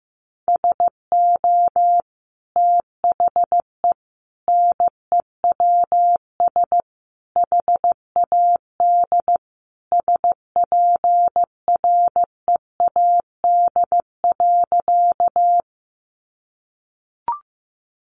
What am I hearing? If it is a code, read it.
Morse code, heard as a keyed tone, about 15 wpm, 700 Hz. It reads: SO THE NEWS HAD SPREAD.
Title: Sentences from Top 1200 Words 15wpm